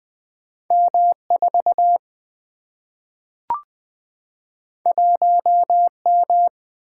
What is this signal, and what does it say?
Morse code, heard as a keyed tone, about 20 wpm, 700 Hz. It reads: M4 1M
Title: Single-Letter Single-Number 20wpm